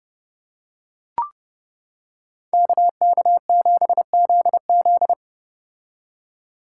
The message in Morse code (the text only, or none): KK7ZZ